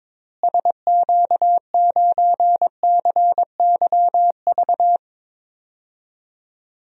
Morse code, heard as a keyed tone, about 22 wpm, 700 Hz. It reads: SQ9CYV
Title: DXCC-100 Call Signs - 6 Characters - Repeated 3x - 22wpm